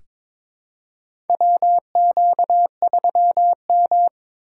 Morse code, heard as a keyed tone, about 22 wpm, 700 Hz. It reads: WQ3M
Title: USA Call Signs - 4 Characters - 2x Character Spacing - 22wpm